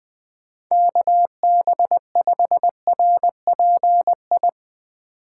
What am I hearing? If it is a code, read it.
Morse code, heard as a keyed tone, about 22 wpm, 700 Hz. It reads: KB5RPI
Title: Speed Racing - USA 2x3 Call Signs 20wpm